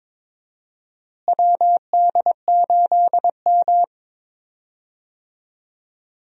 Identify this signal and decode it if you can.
Morse code, heard as a keyed tone, about 22 wpm, 700 Hz. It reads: WD8M
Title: USA Call Signs - 4 Characters - Repeated 3x - 22wpm